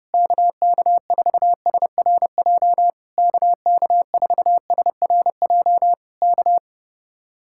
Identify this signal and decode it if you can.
Morse code, heard as a keyed tone, about 28 wpm, 700 Hz. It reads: KK4SRJ KK4SRJ K
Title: Speed-Racing Common QSO Phrases Encore - 30wpm